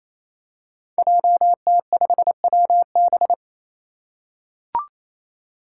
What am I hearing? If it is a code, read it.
Morse code, heard as a keyed tone, about 28 wpm, 700 Hz. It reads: JT5WB
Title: Five Challenging Letter-Number Combinations 28wpm